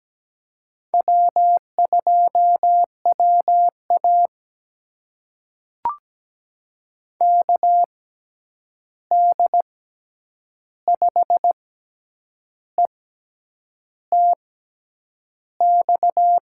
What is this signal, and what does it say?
Morse code, heard as a keyed tone, about 17 wpm, spaced as for 8 wpm, 700 Hz. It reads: W2WA KD5ETX
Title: USA Call Signs - 6x Character Spacing - 17wpm